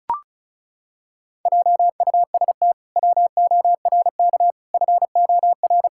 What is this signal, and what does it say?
Morse code, heard as a keyed tone, about 36 wpm, 700 Hz. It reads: JUST WORK FOR
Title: Pangram Challenge v2 - 35wpm